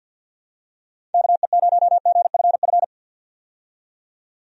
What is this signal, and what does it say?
Morse code, heard as a keyed tone, about 50 wpm, 700 Hz. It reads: KE0GFF
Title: USA Call Signs - 6 Characters - 50wpm